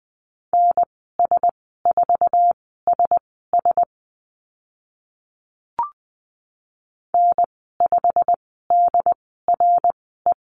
Morse code, heard as a keyed tone, about 20 wpm, spaced as for 14 wpm, 700 Hz. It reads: NS4SS N5DRE
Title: 088 Lesson 15g - Number 5 - Call Signs 20wpm